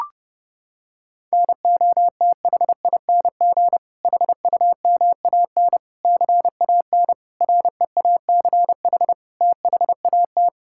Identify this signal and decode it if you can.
Morse code, heard as a keyed tone, about 30 wpm, 700 Hz. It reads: NOTHING HUMAN CAN REACH THAT
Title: Sentences from Top 1600 Words 30wpm